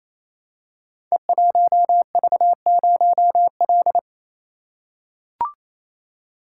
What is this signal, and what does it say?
Morse code, heard as a keyed tone, about 27 wpm, 700 Hz. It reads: E1V0L
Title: Code Groups - 5 Letter:Numbers - 28wpm